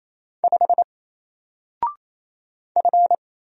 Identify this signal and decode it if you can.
Morse code, heard as a keyed tone, about 28 wpm, 700 Hz. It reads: 5 F
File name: Single-Letter-Number-28wpm